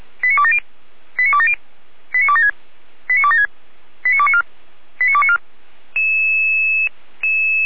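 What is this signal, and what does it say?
Five-tone selective call (ZVEI), heard as quick runs of five tones, tones of about 70 ms. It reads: 89270 89270 89276 89276 89294 89294
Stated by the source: ZVEI five-tone selective call received by SDR, recorded by Feuerwehr Orlamünde.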